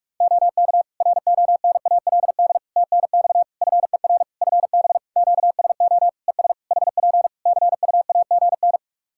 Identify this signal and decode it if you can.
Morse code, heard as a keyed tone, about 42 wpm, 700 Hz. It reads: OK RONALD TNX FER FB QSO ES HP CUAGN
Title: Practice QSO - Exchange 4 - Ending - 45wpm